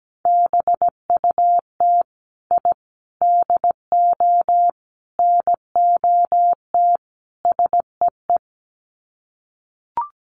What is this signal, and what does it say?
Morse code, heard as a keyed tone, about 17 wpm, 700 Hz. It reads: BUT I DO NOT SEE
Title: Sets of 5 Words Encore 17wpm